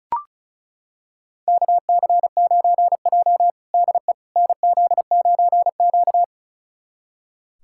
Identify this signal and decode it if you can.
Morse code, heard as a keyed tone, about 36 wpm, 700 Hz. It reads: KC9J DE NZ9Q
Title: USA Callsign DE Callsign - 4 Characters - 35wpm